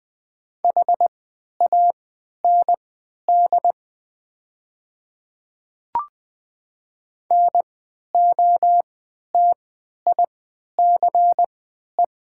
Morse code, heard as a keyed tone, about 21 wpm, spaced as for 12 wpm, 700 Hz. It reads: HAND NOTICE
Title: 3x Character Spacing - Top 500 Words - 20wpm